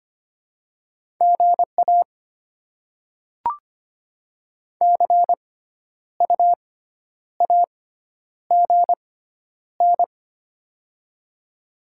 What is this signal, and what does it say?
Morse code, heard as a keyed tone, about 25 wpm, spaced as for 10 wpm, 700 Hz. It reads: GA CU AGN
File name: Top 20 QSO Elements - 6x Character Spacing - 25wpm